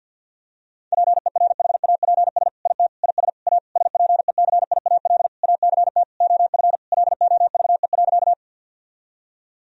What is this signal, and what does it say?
Morse code, heard as a keyed tone, about 47 wpm, 700 Hz. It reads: PERHAPS IT IS A SPECIAL ACT OF LOVE.
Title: Sentences from Top 1000 Words - Review Part 3 - 50wpm